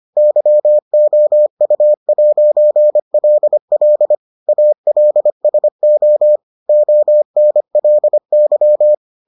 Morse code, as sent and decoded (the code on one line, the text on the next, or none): -.-- --- ..- .----. .-.. .-.. / .- .-.. ... --- / --- -. .-.. -.--
YOU'LL ALSO ONLY